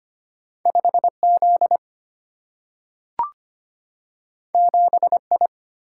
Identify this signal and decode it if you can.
Morse code, heard as a keyed tone, about 25 wpm, 700 Hz. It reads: 5 Z 7 I
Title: Single-Letter Single-Number 25wpm